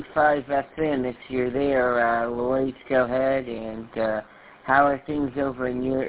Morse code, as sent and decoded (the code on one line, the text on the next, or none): none